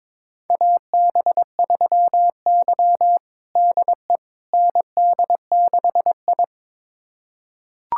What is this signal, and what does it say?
Morse code, heard as a keyed tone, about 22 wpm, 700 Hz. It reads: AB3Y DE ND6I
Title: USA Callsign DE Callsign - 4 Characters - 22wpm